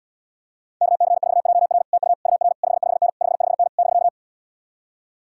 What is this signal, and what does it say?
Morse code, heard as a keyed tone, about 77 wpm, 700 Hz. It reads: KC4JR TU UR 55N 55N <BK>